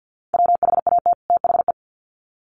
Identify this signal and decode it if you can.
Morse code, heard as a keyed tone, about 50 wpm, 700 Hz. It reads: WHAT THE
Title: Sets of 2 Words 50wpm